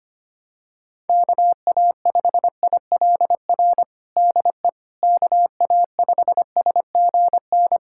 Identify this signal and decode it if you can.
Morse code, heard as a keyed tone, about 25 wpm, 700 Hz. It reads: KA5ILR DE KA5SGN
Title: USA Callsign DE Callsign - 6 Characters - 25wpm